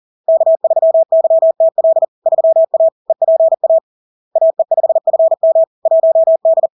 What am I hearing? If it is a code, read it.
Morse code, heard as a keyed tone, about 40 wpm, 650 Hz. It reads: K3YTL 3A EPA AE5FM 1D
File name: LARK-FD-2022-Log-in-CW-40wpm-650-Hz